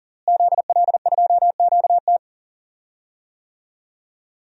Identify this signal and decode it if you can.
Morse code, heard as a keyed tone, about 40 wpm, 700 Hz. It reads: ZL2QT
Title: DXCC-100 Call Signs - 5 Characters - Repeated 3x - 40wpm